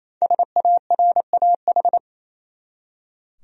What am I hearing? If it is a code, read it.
Morse code, heard as a keyed tone, about 27 wpm, 700 Hz. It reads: SARAH
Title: USA Common Names - Max 5 Letters - 28wpm